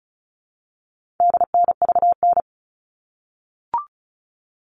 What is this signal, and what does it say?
Morse code, heard as a keyed tone, about 34 wpm, 700 Hz. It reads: DNVN
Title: All Cut Numbers - 4-Digit 35wpm